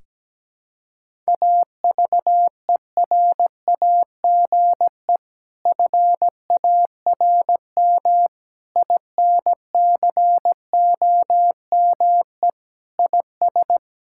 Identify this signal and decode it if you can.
Morse code, heard as a keyed tone, about 17 wpm, 700 Hz. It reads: AVERAGE FARM INCOME IS
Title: Sentences from Top 1000 Words - Review Part 6 - 17wpm